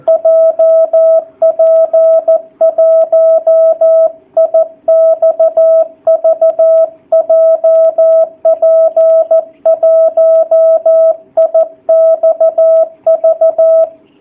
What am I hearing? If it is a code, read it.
Morse code, heard as a keyed tone, about 14 wpm, 660 Hz. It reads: JP1IXV JP1IXV
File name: jh8cbh / 2019cw / sp13- w